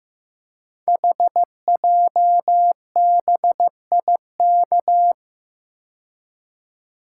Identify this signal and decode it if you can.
Morse code, heard as a keyed tone, about 15 wpm, 700 Hz. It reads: HJBIK
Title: Code Groups - 5 Letters - 15wpm